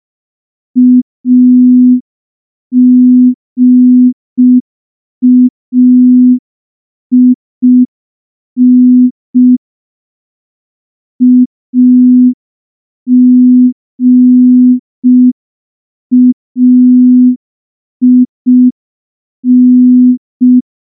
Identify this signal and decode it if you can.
Morse code, heard as a keyed tone, about 5 wpm, 250 Hz. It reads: AGAIN AGAIN
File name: morse / morse-training-05WPM